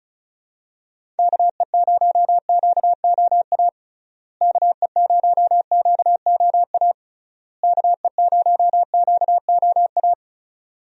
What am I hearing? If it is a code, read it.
Morse code, heard as a keyed tone, about 34 wpm, 700 Hz. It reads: KE0QOA KE0QOA KE0QOA
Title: USA Call Signs - 6 Characters - Repeated 3x - 35wpm